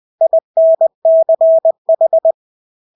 Morse code, heard as a keyed tone, about 20 wpm, 650 Hz. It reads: INCH
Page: Engelske ord 20 wpm